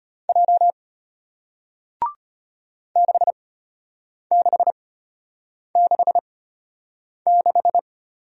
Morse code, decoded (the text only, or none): J 6666